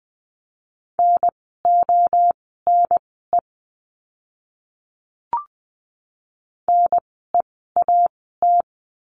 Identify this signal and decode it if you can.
Morse code, heard as a keyed tone, about 20 wpm, spaced as for 14 wpm, 700 Hz. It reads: NONE NEAT
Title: 030 Lesson 6f - Letter i - Words 20wpm